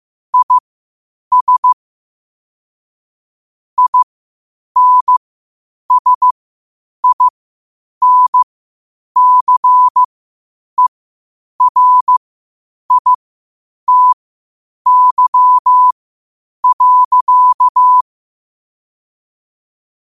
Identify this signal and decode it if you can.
Morse code, heard as a keyed tone, about 15 wpm, spaced as for 8 wpm, 1000 Hz. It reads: IS INSINCERITY.
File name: qotdfiles / 8wpm